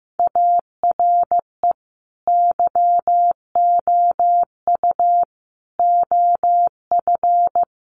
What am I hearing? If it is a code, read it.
Morse code, heard as a keyed tone, about 15 wpm, 700 Hz. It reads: ARE YOU OF
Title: Sentences from Top 1200 Words 15wpm